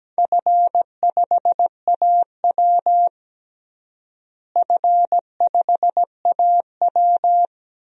Morse code, heard as a keyed tone, about 17 wpm, 700 Hz. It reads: F5AW F5AW
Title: DXCC-100 Call Signs - 4 Characters - Repeated 2x - 17wpm